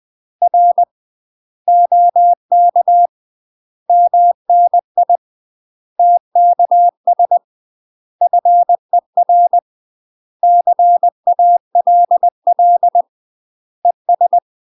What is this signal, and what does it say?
Morse code, heard as a keyed tone, about 20 wpm, 700 Hz. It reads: R OK MNI TKS FER CALL ES
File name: UK MASTER QSOs TRAINING / 20WPM / UK MASTER QSOs 0051